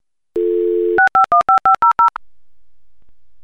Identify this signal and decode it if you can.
DTMF, heard as telephone touch tones, tones of about 120 ms. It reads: 6515500
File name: oldschool-modem-sound